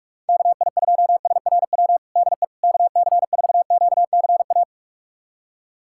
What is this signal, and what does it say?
Morse code, heard as a keyed tone, about 46 wpm, 700 Hz. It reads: KI2SRW DE KC4QCA
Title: USA Callsign DE Callsign 45wpm